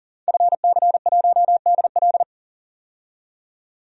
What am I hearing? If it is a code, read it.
Morse code, heard as a keyed tone, about 40 wpm, 700 Hz. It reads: FC1DL
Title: DXCC-100 Call Signs - 5 Characters - 40wpm